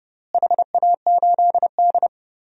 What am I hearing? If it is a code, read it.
Morse code, heard as a keyed tone, about 32 wpm, 700 Hz. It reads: HA8D
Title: DXCC-100 Call Signs - 4 Characters - Repeated 3x - 30wpm